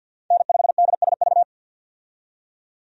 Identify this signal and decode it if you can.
Morse code, heard as a keyed tone, about 52 wpm, 700 Hz. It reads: N5DSV
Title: USA Call Signs - 5 Characters - 50wpm